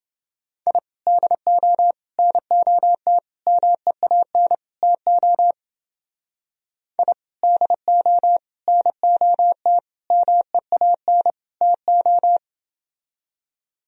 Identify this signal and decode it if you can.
Morse code, heard as a keyed tone, about 28 wpm, 700 Hz. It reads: I DO NOT MEAN TO I DO NOT MEAN TO